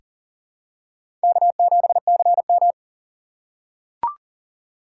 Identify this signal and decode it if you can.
Morse code, heard as a keyed tone, about 40 wpm, 700 Hz. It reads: K7CM
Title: USA Call Signs - 4 Characters - 3x Character Spacing - 40wpm